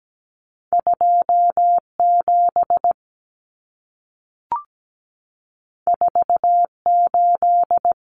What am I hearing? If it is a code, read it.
Morse code, heard as a keyed tone, about 17 wpm, 700 Hz. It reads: 2 7 4 8
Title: Common Cut Numbers - 2-Digit 17wpm